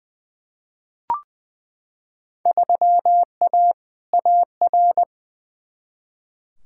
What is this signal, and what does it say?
Morse code, heard as a keyed tone, about 21 wpm, 700 Hz. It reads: 3A AR
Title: Field Day Exchanges Encore - 20wpm